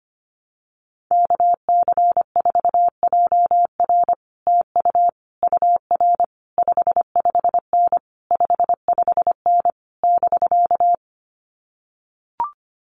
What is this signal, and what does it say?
Morse code, heard as a keyed tone, about 25 wpm, 700 Hz. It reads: KC4JR TU UR 55N 55N <BK>
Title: Common QSO Phrases Encore 25wpm